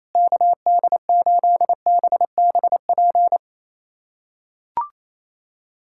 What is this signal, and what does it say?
Morse code, heard as a keyed tone, about 28 wpm, 700 Hz. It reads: KD8BBP
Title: USA Callsign DE Callsign 28wpm